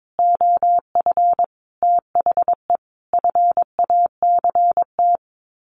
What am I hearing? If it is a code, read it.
Morse code, heard as a keyed tone, about 22 wpm, 700 Hz. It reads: OF THE FACT
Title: Sets of 5 Words Encore2 22wpm